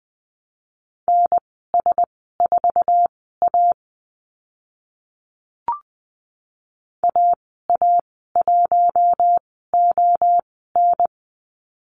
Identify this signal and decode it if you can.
Morse code, heard as a keyed tone, about 20 wpm, spaced as for 15 wpm, 700 Hz. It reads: NS4A AA1ON
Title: 055 Lesson 10g - Letter r - Call Signs 20wpm